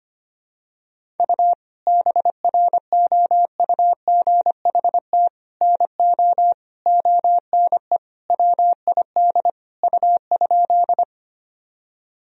Morse code, heard as a keyed tone, about 25 wpm, 700 Hz. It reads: U BROUGHT NO ONE WID U?
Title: Sentences from Top 500 Words and Abbreviations 25wpm